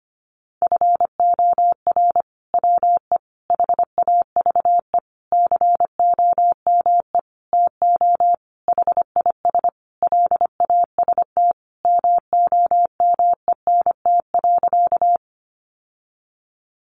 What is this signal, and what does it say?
Morse code, heard as a keyed tone, about 25 wpm, 700 Hz. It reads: FOR WE HAVE COME TO HIS LAST MOMENT.
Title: Sentences from Top 1000 Words 25wpm